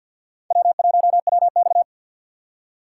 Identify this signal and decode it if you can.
Morse code, heard as a keyed tone, about 53 wpm, 700 Hz. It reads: W1WX
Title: USA Call Signs - 4 Characters - 2x Character Spacing - 50wpm